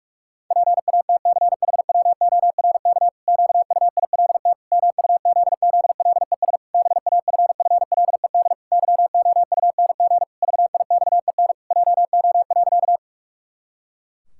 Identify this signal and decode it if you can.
Morse code, heard as a keyed tone, about 46 wpm, 700 Hz. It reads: PATCHWORK QUILT MUZZLES BAFFLED YOUNG VIXEN JO.
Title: Pangram Challenge v2 - 45wpm